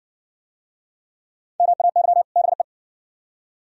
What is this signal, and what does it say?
Morse code, heard as a keyed tone, about 58 wpm, 700 Hz. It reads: MAY BE